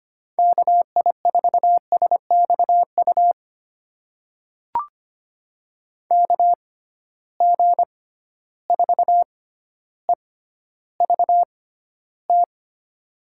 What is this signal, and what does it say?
Morse code, heard as a keyed tone, about 25 wpm, spaced as for 13 wpm, 700 Hz. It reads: KI4SXU KG4EVT
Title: USA Call Signs - 6x Character Spacing - 25wpm